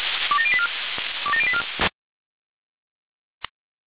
Five-tone selective call (ZVEI), heard as quick runs of five tones, tones of about 70 ms. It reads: 27084 27084